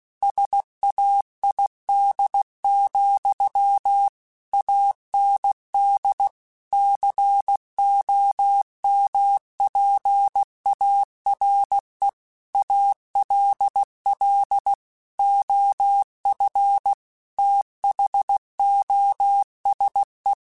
Morse code, heard as a keyed tone, about 16 wpm, 800 Hz. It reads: SAID, AND COMPARE ALL OF THOSE
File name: Technical Subjects / Technical Subjects-mp3s / OR Gate @15